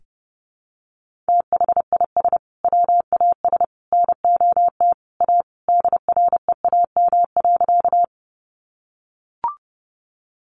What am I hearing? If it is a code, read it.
Morse code, heard as a keyed tone, about 30 wpm, 700 Hz. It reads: THIS WAS NOT A DREAM.
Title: Sentences from Top 1500 Words 30wpm